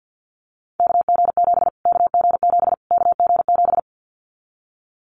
Morse code, heard as a keyed tone, about 50 wpm, 700 Hz. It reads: KG7 KG7 KG7
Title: Callsign Prefixes for Top 25 DXCC - Repeated 3x - 50wpm